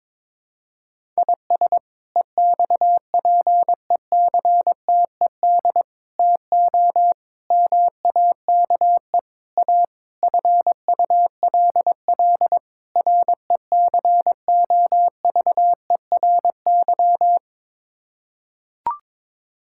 Morse code, heard as a keyed tone, about 22 wpm, 700 Hz. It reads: IS EXPECTED TO MAKE A FULL RECOVERY
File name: practice / 22wpm / Sets of 7 Words - 22wpm